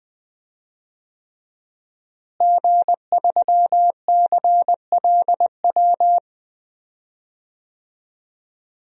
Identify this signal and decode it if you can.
Morse code, heard as a keyed tone, about 20 wpm, 700 Hz. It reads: G3CLW
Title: DXCC-100 Call Signs - 5 Characters - Repeated 3x - 20wpm